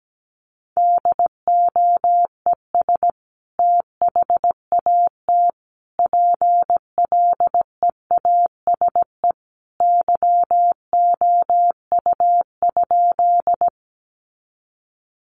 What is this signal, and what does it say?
Morse code, heard as a keyed tone, about 17 wpm, 700 Hz. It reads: DOES THAT PLEASE YOU?
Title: Sentences from Top 1500 Words 17wpm